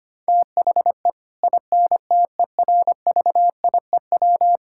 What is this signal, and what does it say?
Morse code, heard as a keyed tone, about 25 wpm, 700 Hz. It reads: THE INTERVIEW
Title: Sentences from Top 1600 Words 25wpm